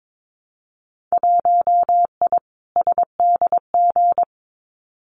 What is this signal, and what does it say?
Morse code, heard as a keyed tone, about 22 wpm, 700 Hz. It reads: 1I SDG
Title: Winter Field Day Exchanges 22wpm